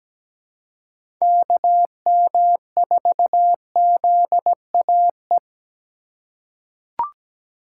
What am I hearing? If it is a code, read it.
Morse code, heard as a keyed tone, about 17 wpm, 700 Hz. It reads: KM4ZAE
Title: USA Call Signs - 3x Character Spacing - 17wpm